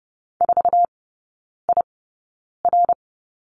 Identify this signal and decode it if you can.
Morse code, heard as a keyed tone, about 30 wpm, spaced as for 9 wpm, 690 Hz. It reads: 4IR
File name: Lesson 16e - Letter c - Three Characters 30wpm